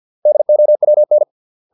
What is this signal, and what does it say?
Morse code, heard as a keyed tone, about 50 wpm, 600 Hz. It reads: DOWN